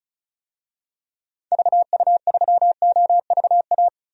Morse code, heard as a keyed tone, about 34 wpm, 700 Hz. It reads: VU3OVA
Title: DXCC-100 Call Signs - 6 Characters - 35wpm